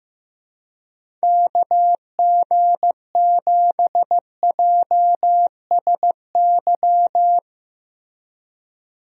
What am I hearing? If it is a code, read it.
Morse code, heard as a keyed tone, about 15 wpm, 700 Hz. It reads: KG7JSY
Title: USA Call Signs - 6 Characters - Repeated 2x - 15wpm